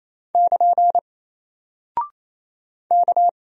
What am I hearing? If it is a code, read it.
Morse code, heard as a keyed tone, about 28 wpm, 700 Hz. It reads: <KN> K
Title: Single Character (Letter:Number:Punctuation:Prosign) Alt 28wpm